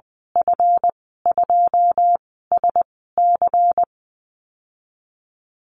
Morse code, heard as a keyed tone, about 20 wpm, 700 Hz. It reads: F 2 S C
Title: 173 Lesson 28g - Number 9 - Call Signs 20wpm